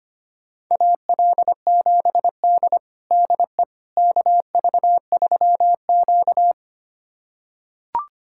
Morse code, heard as a keyed tone, about 25 wpm, 700 Hz. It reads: AL7D DE KV3Q
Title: USA Callsign DE Callsign - 4 Characters - 25wpm